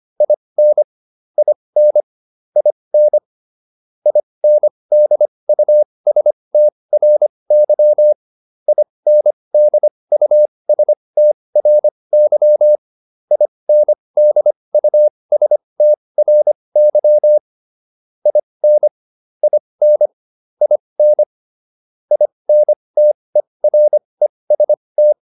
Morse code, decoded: IN IN IN INDUSTRY INDUSTRY INDUSTRY IN IN IN INTEREST

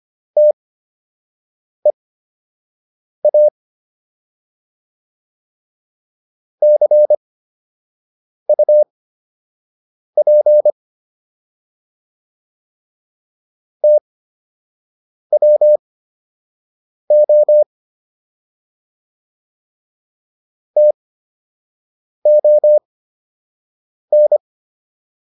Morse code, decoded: TEA CUP TWO TON

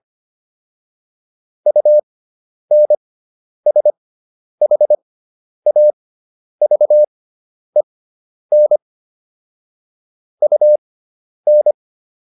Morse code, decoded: UNSHAVEN UN

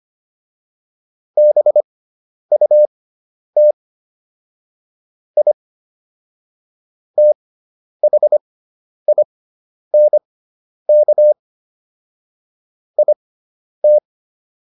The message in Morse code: BUT I THINK IT

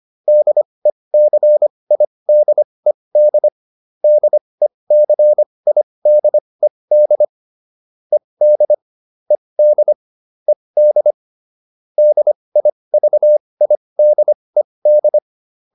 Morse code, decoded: DECIDED DECIDED ED ED ED DIVIDED

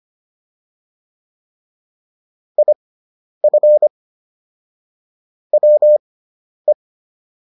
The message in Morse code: IF WE

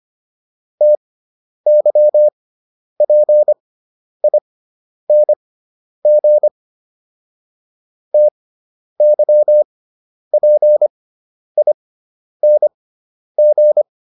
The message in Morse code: TYPING TYPING

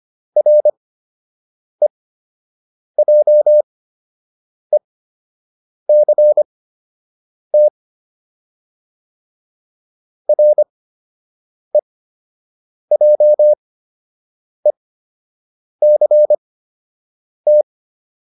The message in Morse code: REJECT REJECT

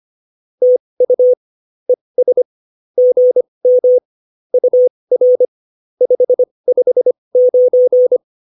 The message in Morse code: TU ES GM UR 559

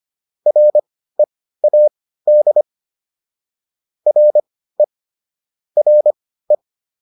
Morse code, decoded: READ RE RE

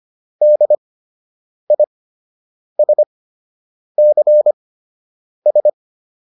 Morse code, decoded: DISCS

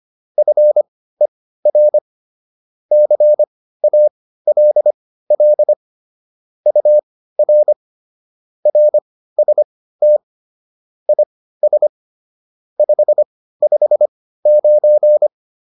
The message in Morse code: FER CALL UR RST IS 559

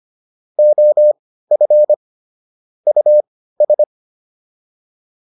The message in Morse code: OF US

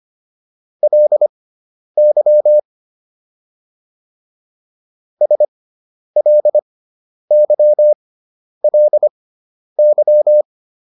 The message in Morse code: LY SLYLY

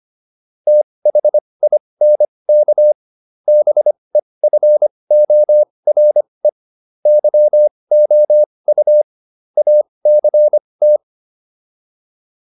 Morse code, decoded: THINK BEFORE YOU ACT

